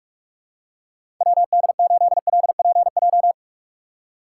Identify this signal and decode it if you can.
Morse code, heard as a keyed tone, about 46 wpm, 700 Hz. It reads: WD8LPJ
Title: USA Call Signs - 6 Characters - 3x Character Spacing - 45wpm